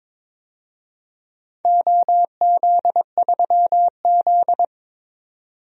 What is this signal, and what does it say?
Morse code, heard as a keyed tone, about 22 wpm, 700 Hz. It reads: OZ3Z
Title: DXCC-100 Call Signs - 4 Characters - 22wpm